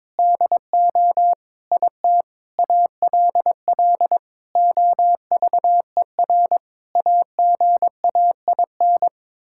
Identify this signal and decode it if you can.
Morse code, heard as a keyed tone, about 22 wpm, 700 Hz. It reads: DO IT ALL OVER AGAIN
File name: Sets of 9 Words - 22wpm